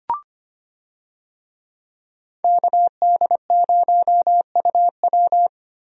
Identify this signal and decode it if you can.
Morse code, heard as a keyed tone, about 25 wpm, 700 Hz. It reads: KD0UW
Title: USA Call Signs - 5 Characters - Repeated 2x - 25wpm